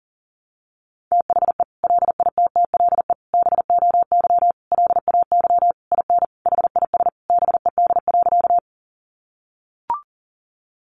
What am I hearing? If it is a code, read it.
Morse code, heard as a keyed tone, about 40 wpm, 700 Hz. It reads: THE LITTLE BOY LAY IN HIS BED.
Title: Sentences from Top 500 Words 40wpm